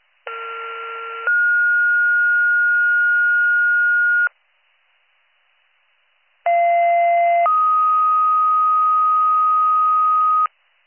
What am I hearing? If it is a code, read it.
Two-tone page, heard as a one-second tone, then a three-second tone, about 1370 then 1430 Hz, about 690 then 1250 Hz. It is heard from 0.3 to 10.5 s.